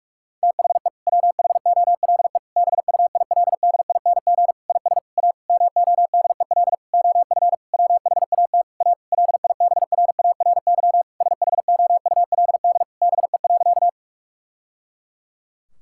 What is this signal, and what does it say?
Morse code, heard as a keyed tone, about 46 wpm, 700 Hz. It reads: THE WHOLE BUILDING IS A MODEL OF WHAT A LIBRARY SHOULD BE.
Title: Sentences from Top 1000 Words - Review Part 5 - 45wpm